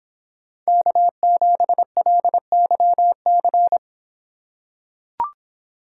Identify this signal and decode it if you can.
Morse code, heard as a keyed tone, about 26 wpm, 700 Hz. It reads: K7LYC